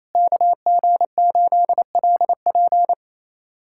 Morse code, heard as a keyed tone, about 27 wpm, 700 Hz. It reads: KG8LP
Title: USA Callsign DE Callsign 28wpm